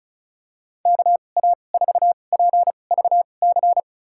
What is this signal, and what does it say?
Morse code, heard as a keyed tone, about 34 wpm, spaced as for 25 wpm, 700 Hz. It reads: KA4PVC
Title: USA Call Signs - 6 Characters - 2x Character Spacing - 35wpm